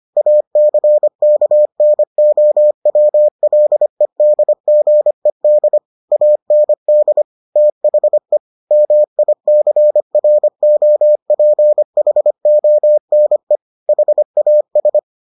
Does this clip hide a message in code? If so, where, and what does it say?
Yes: Morse code from 0.2 to 15.0 s, heard as a keyed tone, about 25 wpm, 600 Hz. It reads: ACKNOWLEDGED AND THE MICROPHONE HAS